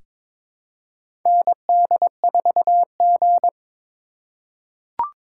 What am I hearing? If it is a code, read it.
Morse code, heard as a keyed tone, about 22 wpm, 700 Hz. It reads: ND4G
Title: K1USN SST QSO Practice - 22wpm